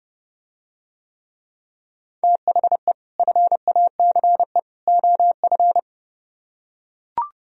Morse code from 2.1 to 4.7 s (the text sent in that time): THE FACE